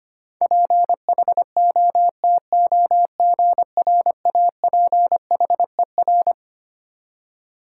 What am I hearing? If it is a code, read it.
Morse code, heard as a keyed tone, about 25 wpm, 700 Hz. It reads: PHOTOGRAPHER
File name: Twelve Letter Words - 25wpm